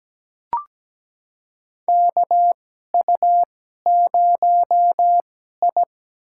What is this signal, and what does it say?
Morse code, heard as a keyed tone, about 17 wpm, spaced as for 12 wpm, 700 Hz. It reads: KU0I
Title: USA Call Signs - 4 Characters - 2x Character Spacing - 17wpm